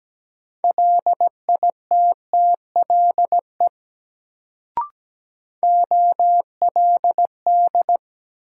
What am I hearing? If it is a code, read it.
Morse code, heard as a keyed tone, about 17 wpm, 700 Hz. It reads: LITTLE OLD
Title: Sets of 2 Words - New Word Drill - 17wpm